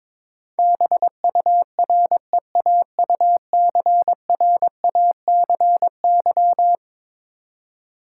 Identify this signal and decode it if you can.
Morse code, heard as a keyed tone, about 22 wpm, 700 Hz. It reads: BUREAUCRACY
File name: practice / 22wpm / Eleven Letter Words - 22wpm